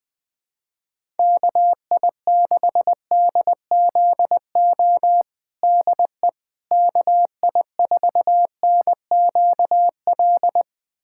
Morse code, heard as a keyed tone, about 20 wpm, 700 Hz. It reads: KI6DZO DE KI4NQL
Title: USA Callsign DE Callsign 20wpm